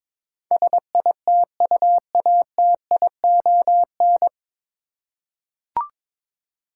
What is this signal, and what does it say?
Morse code, heard as a keyed tone, about 22 wpm, 700 Hz. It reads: SITUATION
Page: Morse Code Ninja practice 22wpm